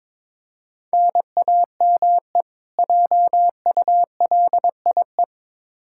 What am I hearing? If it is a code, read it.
Morse code, heard as a keyed tone, about 22 wpm, 700 Hz. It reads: NAME JULIE
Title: K1USN SST QSO Practice 22wpm